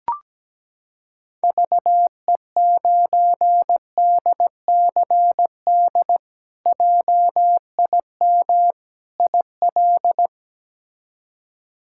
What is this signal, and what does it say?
Morse code, heard as a keyed tone, about 17 wpm, 700 Hz. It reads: VE9DCD JIM IL
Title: K1USN SST QSO Practice - 17wpm